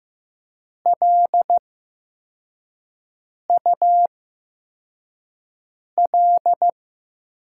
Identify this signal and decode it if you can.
Morse code, heard as a keyed tone, about 15 wpm, spaced as for 4 wpm, 700 Hz. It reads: LUL
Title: USA Call Signs - 6 Characters - 8x Character Spacing - 15wpm